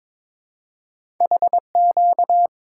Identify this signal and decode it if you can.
Morse code, heard as a keyed tone, about 22 wpm, 700 Hz. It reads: HQ
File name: practice / 22wpm / Two Letter-Number Combinations - 22wpm